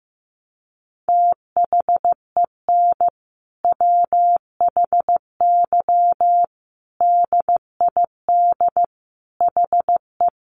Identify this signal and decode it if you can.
Morse code, heard as a keyed tone, about 15 wpm, 700 Hz. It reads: THEN WHY DID HE
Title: Sentences from Top 800 Words 15wpm